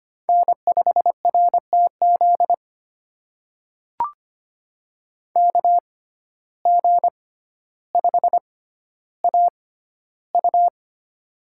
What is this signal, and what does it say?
Morse code, heard as a keyed tone, about 25 wpm, spaced as for 13 wpm, 700 Hz. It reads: N5RTZ KG5AU
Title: USA Call Signs - 5 Characters - 6x Character Spacing - 25wpm